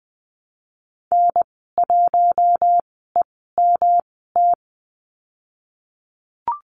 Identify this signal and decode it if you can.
Morse code, heard as a keyed tone, about 20 wpm, spaced as for 15 wpm, 700 Hz. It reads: N1EMT
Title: 147 Lesson 24g - Letter y - Call Signs 20wpm